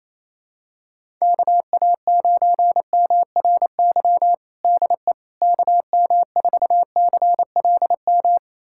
Morse code, heard as a keyed tone, about 28 wpm, 700 Hz. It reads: KA9MRY DE KM4CLM
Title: USA Callsign DE Callsign - 6 Characters - 28wpm